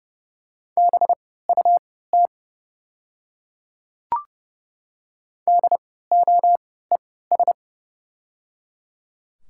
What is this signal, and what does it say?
Morse code, heard as a keyed tone, about 30 wpm, spaced as for 17 wpm, 700 Hz. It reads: BUT DOES
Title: 3x Character Spacing - Top 500 Words - 30wpm